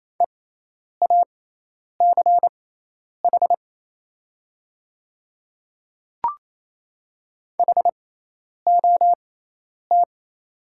Morse code, heard as a keyed tone, about 27 wpm, spaced as for 10 wpm, 700 Hz. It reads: EACH HOT
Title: 6x Character Spacing - Top 100 Words - 28wpm